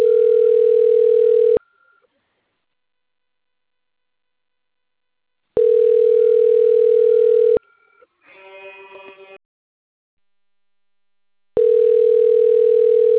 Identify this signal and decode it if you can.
Ringback tone (North America), 440 + 480 Hz, with speech.
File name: rg-1600-unknown- - - .9603